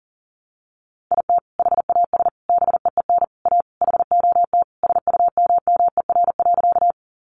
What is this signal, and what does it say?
Morse code, heard as a keyed tone, about 40 wpm, 700 Hz. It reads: IT HAS BEEN A HOT SUMMER.
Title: Sentences from Top 1100 Words 40wpm